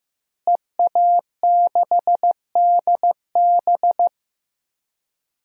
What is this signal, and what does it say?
Morse code, heard as a keyed tone, about 15 wpm, 700 Hz. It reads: EA6DB
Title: DXCC-100 Call Signs - 5 Characters - Repeated 3x - 15wpm